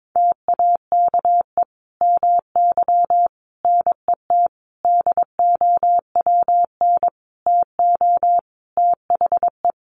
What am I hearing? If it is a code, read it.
Morse code, heard as a keyed tone, about 22 wpm, 700 Hz. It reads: TAKE MY NET DOWN TO THE
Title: Sentences from Top 1400 Words 22wpm